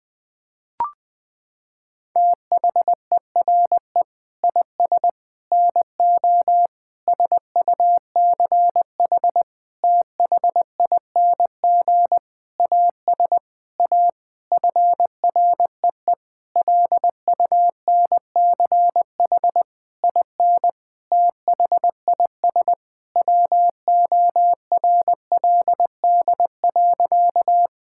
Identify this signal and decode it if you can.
Morse code, heard as a keyed tone, about 20 wpm, 700 Hz. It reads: THERE IS NO SUCH THING AS A FREE LUNCH IN THIS WORLD.
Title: Sentences with 9-Word Phrases - 20wpm